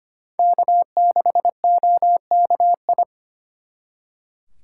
Morse code, heard as a keyed tone, about 32 wpm, 700 Hz. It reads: K6OKI